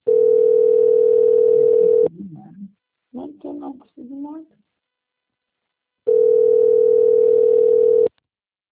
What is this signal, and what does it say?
Ringback tone (North America), 440 + 480 Hz, with speech.
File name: out- - - - - .4730